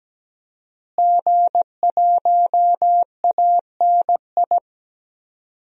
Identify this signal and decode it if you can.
Morse code, heard as a keyed tone, about 17 wpm, 700 Hz. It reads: G1ANI
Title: DXCC-100 Call Signs - 5 Characters - Repeated 2x - 17wpm